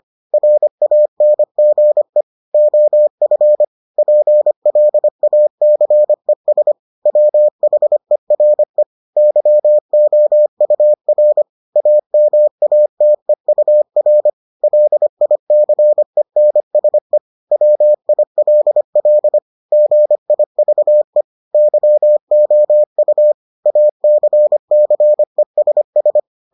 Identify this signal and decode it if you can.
Morse code, heard as a keyed tone, about 25 wpm, 600 Hz. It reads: RANGE OF PLACES WHERE YOUR AMATEUR LICENSE WILL GIVE YOU ACCESS